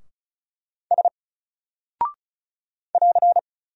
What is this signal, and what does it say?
Morse code, heard as a keyed tone, about 36 wpm, 700 Hz. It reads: S <AR>
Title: Single Character (Letter:Number:Punctuation:Prosign) 35wpm